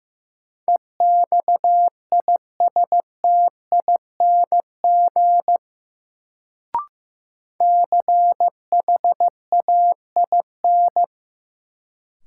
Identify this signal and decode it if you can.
Morse code, heard as a keyed tone, about 15 wpm, 700 Hz. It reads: EXISTING CHAIN
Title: Sentences from Top 1500 Words - New Word Drill - 15wpm